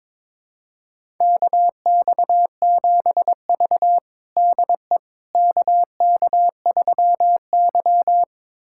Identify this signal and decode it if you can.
Morse code, heard as a keyed tone, about 23 wpm, 700 Hz. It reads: KX7V DE KK3Y
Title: USA Callsign DE Callsign - 4 Characters - 22wpm